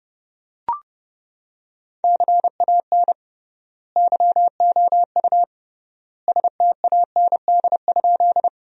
Morse code, heard as a keyed tone, about 28 wpm, 700 Hz. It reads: CAN YOU STAND?
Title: 3x Word-Spacing - Sentences from Top 300 Words - 30wpm